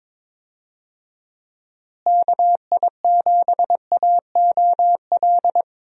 Morse code, heard as a keyed tone, about 21 wpm, 700 Hz. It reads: KI7AOL
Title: USA Call Signs - 6 Characters - Repeated 3x - 22wpm